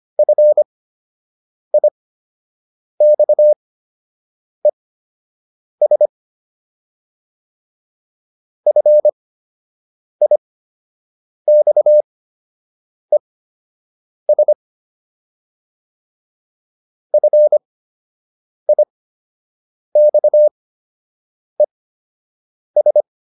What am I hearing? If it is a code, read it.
Morse code, heard as a keyed tone, about 25 wpm, spaced as for 7 wpm, 600 Hz. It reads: FIXES FIXES FIXES